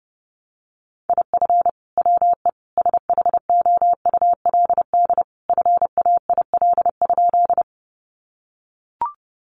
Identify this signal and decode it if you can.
Morse code, heard as a keyed tone, about 30 wpm, 700 Hz. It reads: IF WE SHOULD FAIL?
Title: Sentences from Top 1100 Words 30wpm